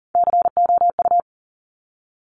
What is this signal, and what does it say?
Morse code, heard as a keyed tone, about 42 wpm, 700 Hz. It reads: COU
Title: Top Three Letter Combinations in Words 40wpm